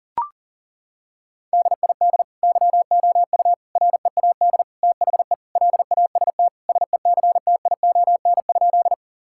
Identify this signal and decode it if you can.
Morse code, heard as a keyed tone, about 37 wpm, 700 Hz. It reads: DID YOU READ THE LAST SECTION?
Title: Sentences from Top 1000 Words - Review Part 5 - 40wpm